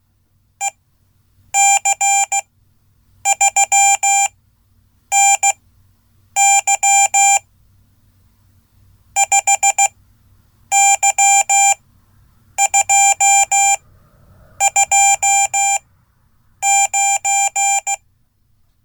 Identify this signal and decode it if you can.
Morse code, heard as a keyed tone, about 16 wpm, spaced as for 8 wpm, 790 Hz. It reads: EC3NY 5Y229
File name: Code / 5min-a